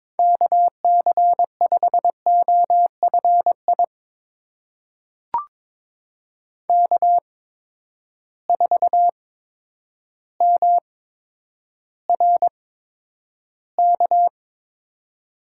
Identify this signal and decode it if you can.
Morse code, heard as a keyed tone, about 22 wpm, spaced as for 10 wpm, 700 Hz. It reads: KC5OFI K4MRK
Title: USA Call Signs - 8x Character Spacing - 22wpm